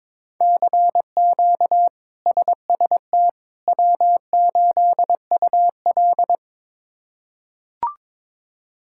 Morse code, heard as a keyed tone, about 22 wpm, 700 Hz. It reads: CQ SST W8UL
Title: K1USN SST QSO Practice - 22wpm